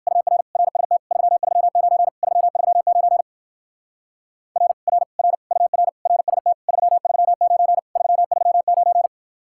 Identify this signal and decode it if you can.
Morse code, heard as a keyed tone, about 57 wpm, 700 Hz. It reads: UR RST 339 339 R R R UR RST 339 339